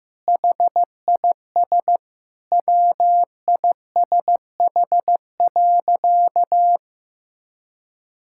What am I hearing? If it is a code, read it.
Morse code, heard as a keyed tone, about 16 wpm, 700 Hz. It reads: HIS WISH.
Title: Sentences from Top 1600 Words 15wpm